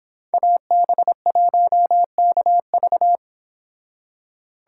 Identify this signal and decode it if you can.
Morse code, heard as a keyed tone, about 26 wpm, 700 Hz. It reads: AB1KV